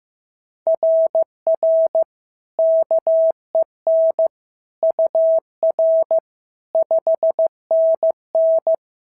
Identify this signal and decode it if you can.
Morse code, heard as a keyed tone, about 14 wpm, 650 Hz. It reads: RR KEN UR 5NN
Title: SOTA QSO Practice 15wpm